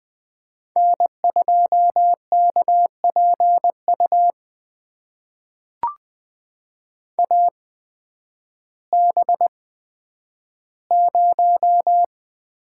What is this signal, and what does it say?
Morse code, heard as a keyed tone, about 20 wpm, spaced as for 11 wpm, 700 Hz. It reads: N2KPU AB0
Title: USA Call Signs - 5 Characters - 8x Character Spacing - 20wpm